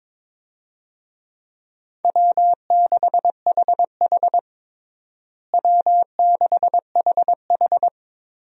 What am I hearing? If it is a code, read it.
Morse code, heard as a keyed tone, about 22 wpm, 700 Hz. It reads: W6HH W6HH
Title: USA Call Signs - 4 Characters - Repeated 2x - 22wpm